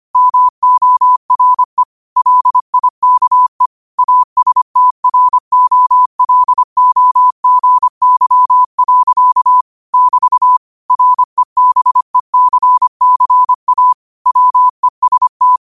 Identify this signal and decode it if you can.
Morse code, heard as a keyed tone, about 25 wpm, 1000 Hz. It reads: MORE LIKE ASTROLOGY. <BT> REBECCA WEST
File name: qotdfiles / 25wpm